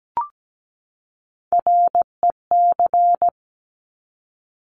Morse code, heard as a keyed tone, about 17 wpm, 690 Hz. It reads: REC
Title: Top Three Letter Combinations in Words 17wpm